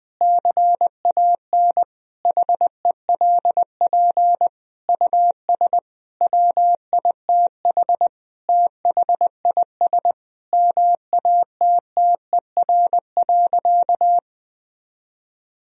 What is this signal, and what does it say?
Morse code, heard as a keyed tone, about 20 wpm, 700 Hz. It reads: CAN HELP US WITH THIS MATTER.
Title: Sentences from Top 1000 Words - Review Part 2 - 20wpm